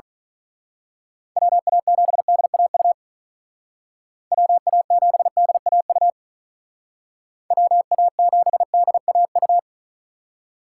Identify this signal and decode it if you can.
Morse code, heard as a keyed tone, about 41 wpm, 700 Hz. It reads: WA7DAU WA7DAU WA7DAU
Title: Speed Racing - USA 2x3 Call Signs 35wpm